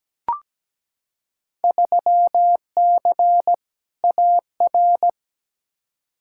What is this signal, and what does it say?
Morse code, heard as a keyed tone, about 17 wpm, 700 Hz. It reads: 3C AR
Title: Field Day Exchanges Encore - 17wpm